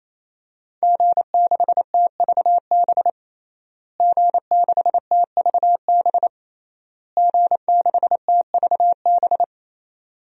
Morse code, heard as a keyed tone, about 28 wpm, 700 Hz. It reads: G6TVB G6TVB G6TVB
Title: DXCC-100 Call Signs - 5 Characters - Repeated 3x - 28wpm